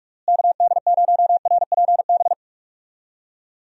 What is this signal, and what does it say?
Morse code, heard as a keyed tone, about 46 wpm, 700 Hz. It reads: KD0RPB
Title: USA Call Signs - 6 Characters - Repeated 3x - 45wpm